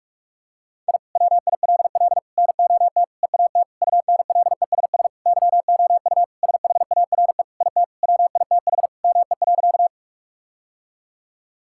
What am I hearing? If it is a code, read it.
Morse code, heard as a keyed tone, about 46 wpm, 700 Hz. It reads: I WILL NOT EAT UNLESS YOU SHARE IT WITH ME.
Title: Sentences from Top 1000 Words - Review Part 6 - 45wpm